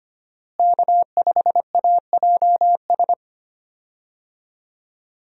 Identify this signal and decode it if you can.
Morse code, heard as a keyed tone, about 25 wpm, 700 Hz. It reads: K5AJS
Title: USA Call Signs - 5 Characters - Repeated 3x - 25wpm